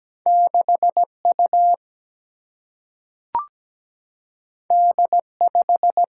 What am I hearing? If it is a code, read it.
Morse code, heard as a keyed tone, about 17 wpm, 700 Hz. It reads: 6 U D 5
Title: Single-Letter Single-Number 17wpm